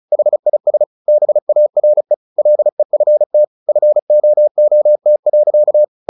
Morse code, heard as a keyed tone, about 36 wpm, 600 Hz. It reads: HIS BARE LEFT FOOT.
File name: the old man and the sea 2 35wpm